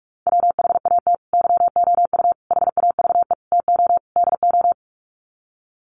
Morse code, heard as a keyed tone, about 44 wpm, 700 Hz. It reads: WHAT YOU HAVE TO DO
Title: Sets of 5 Words Encore 45wpm